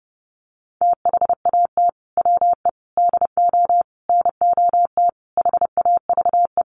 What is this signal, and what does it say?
Morse code, heard as a keyed tone, about 30 wpm, 700 Hz. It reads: THAT WE DO NOT HAVE
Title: Sets of 5 Words Encore 30wpm